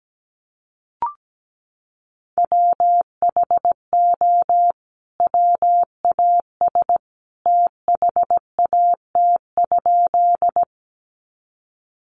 Morse code, heard as a keyed tone, about 17 wpm, 700 Hz. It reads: WHO WAS THAT?
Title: Sentences from Top 100 Words 17wpm